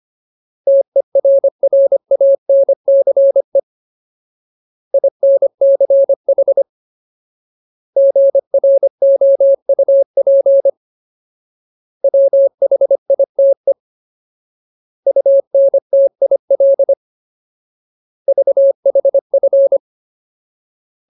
Morse code, heard as a keyed tone, about 25 wpm, 550 Hz. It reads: TERRANCE INCH GROUP WHITE UNTIL VHF
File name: Ultimate Word List / 25WPM Ultimate